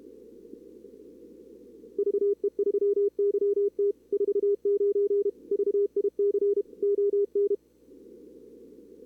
Morse code, heard as a keyed tone, about 28 wpm, 440 Hz. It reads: VE3YT 49 VIC ON
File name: CWSprint / Feb2022 / k5tr-feb-na-sprint-cw-k5ot -2300